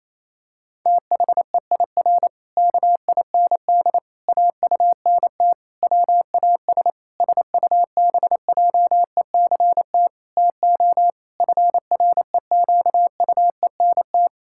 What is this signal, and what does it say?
Morse code, heard as a keyed tone, about 28 wpm, 700 Hz. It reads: THEIR KIND AUNT WAS SUBJECT TO FREQUENT
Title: Pangram Challenge v2 - 28wpm